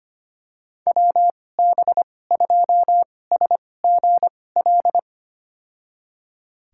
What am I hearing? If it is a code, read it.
Morse code, heard as a keyed tone, about 25 wpm, spaced as for 18 wpm, 700 Hz. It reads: WB2SGL
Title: USA Call Signs - 6 Characters - 2x Character Spacing - 25wpm